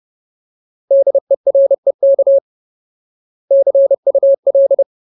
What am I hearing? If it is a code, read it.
Morse code, heard as a keyed tone, about 30 wpm, 550 Hz. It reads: DEREK CUL
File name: Ultimate Word List / 30WPM Ultimate